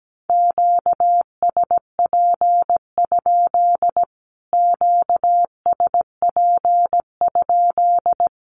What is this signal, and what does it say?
Morse code, heard as a keyed tone, about 17 wpm, 700 Hz. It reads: QSP? QSP?
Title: Q-Codes Repeated 3x 17wpm